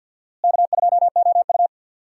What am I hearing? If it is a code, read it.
Morse code, heard as a keyed tone, about 47 wpm, 700 Hz. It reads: K2OU
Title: USA Call Signs - 4 Characters - Repeated 3x - 50wpm